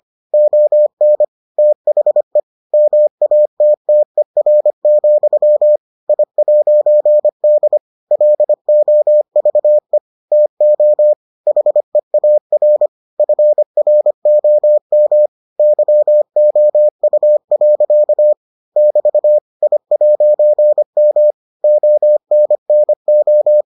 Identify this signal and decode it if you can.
Morse code, heard as a keyed tone, about 25 wpm, 600 Hz. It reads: ON THE MATTER, I'D LOVE TO HEAR FROM YOU. <BT> I'M ONNO